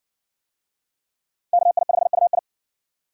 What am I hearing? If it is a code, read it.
Morse code, heard as a keyed tone, about 60 wpm, 700 Hz. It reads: KI5UI